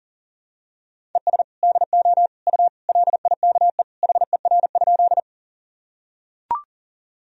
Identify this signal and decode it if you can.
Morse code, heard as a keyed tone, about 40 wpm, 700 Hz. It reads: ES DO U LIKE HER?
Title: Sentences from Top 100 Words and Abbreviations 40wpm